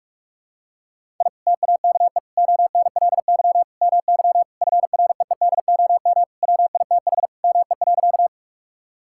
Morse code, heard as a keyed tone, about 46 wpm, 700 Hz. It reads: I TAKE ONLY MY FREEDOM WITH ME.
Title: Sentences from Top 1000 Words - Review Part 5 - 45wpm